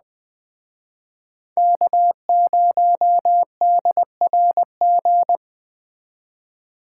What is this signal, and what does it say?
Morse code, heard as a keyed tone, about 20 wpm, 700 Hz. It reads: K0DRG
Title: Speed Racing - USA 1x3 Call Signs 17wpm